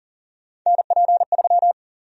Morse code, heard as a keyed tone, about 40 wpm, 700 Hz. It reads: NP3
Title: USA Call Sign Prefixes - Repeated 2x - 40wpm